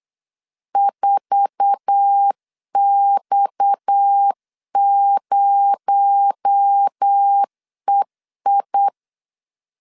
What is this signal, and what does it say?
Morse code, heard as a keyed tone, about 8 wpm, 800 Hz. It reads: 4X0EI
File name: LEZIONI / 8 YJ45 / 8 YJ4 8wpm 3246